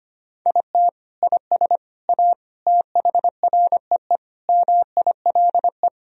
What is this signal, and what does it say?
Morse code, heard as a keyed tone, about 25 wpm, 700 Hz. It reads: IT IS A THREE MILE
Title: Sentences from Top 1000 Words - Review Part 1 - 25wpm